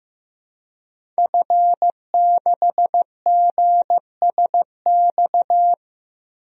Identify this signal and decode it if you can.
Morse code, heard as a keyed tone, about 15 wpm, 700 Hz. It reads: F6GSX
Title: DXCC-100 Call Signs - 5 Characters - Repeated 2x - 15wpm